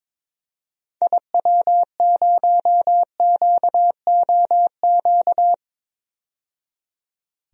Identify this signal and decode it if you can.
Morse code, heard as a keyed tone, about 22 wpm, 700 Hz. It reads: IW0QOQ
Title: DXCC-100 Call Signs - 6 Characters - 22wpm